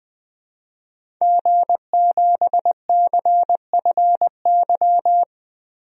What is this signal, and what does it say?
Morse code, heard as a keyed tone, about 20 wpm, 700 Hz. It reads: G7CFY
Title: DXCC-100 Call Signs - 5 Characters - 20wpm